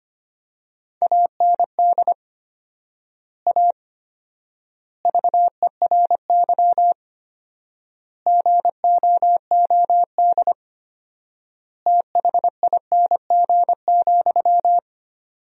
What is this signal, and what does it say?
Morse code, heard as a keyed tone, about 25 wpm, 700 Hz. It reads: AND A VERY GOOD THING,
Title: 4x Word-Spacing - Sentences from Top 200 Words - 25wpm